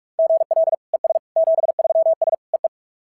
Morse code, heard as a keyed tone, about 44 wpm, 660 Hz. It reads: GL ES 73S EE
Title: SOTA QSO Practice 45wpm